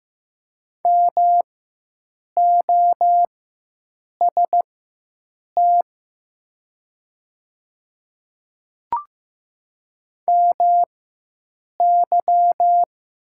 Morse code, heard as a keyed tone, about 15 wpm, spaced as for 7 wpm, 700 Hz. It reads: MOST MY
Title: 4x Character Spacing - Top 100 Words - 15wpm